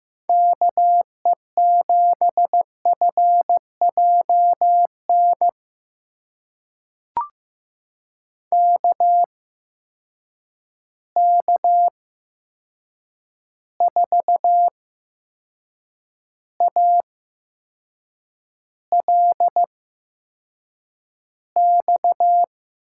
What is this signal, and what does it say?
Morse code, heard as a keyed tone, about 15 wpm, spaced as for 6 wpm, 700 Hz. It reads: KE7FJN KK4ALX
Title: USA Call Signs - 6 Characters - 8x Character Spacing - 15wpm